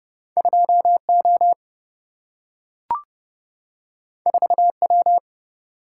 Morse code, heard as a keyed tone, about 31 wpm, 700 Hz. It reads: 2O 4W
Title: Single-Letter Single-Number 30wpm